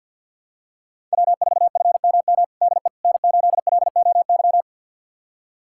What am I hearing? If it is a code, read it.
Morse code, heard as a keyed tone, about 50 wpm, 700 Hz. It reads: W4FMM DE N8LOY
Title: USA Callsign DE Callsign - 5 Characters - 50wpm